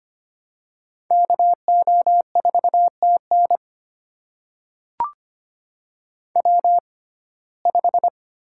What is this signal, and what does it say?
Morse code, heard as a keyed tone, about 25 wpm, spaced as for 18 wpm, 700 Hz. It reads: KO4TN W5
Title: USA Call Signs - 5 Characters - 6x Character Spacing - 25wpm